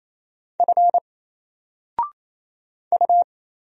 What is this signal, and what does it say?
Morse code, heard as a keyed tone, about 28 wpm, 700 Hz. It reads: F U
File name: Single-Letter-Number-28wpm